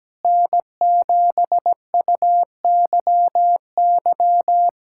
Morse code, heard as a keyed tone, about 17 wpm, 700 Hz. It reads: N7UYY
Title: USA Callsign DE Callsign 17wpm